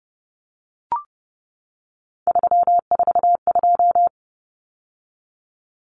Morse code, heard as a keyed tone, about 30 wpm, 700 Hz. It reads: 342
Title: Numbers - 3-Digit 30wpm